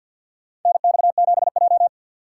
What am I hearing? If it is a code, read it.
Morse code, heard as a keyed tone, about 47 wpm, 700 Hz. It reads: NX7J
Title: USA Call Signs - 4 Characters - 2x Character Spacing - 50wpm